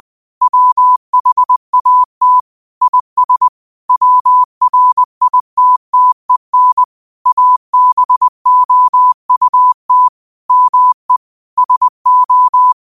Morse code, heard as a keyed tone, about 20 wpm, 1000 Hz. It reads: WHAT IS WRITTEN ABOUT ME SO